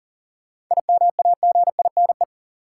Morse code, heard as a keyed tone, about 40 wpm, 700 Hz. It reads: IMAGINE
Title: Sentences from Top 1400 Words - New Word Drill Alt - 40wpm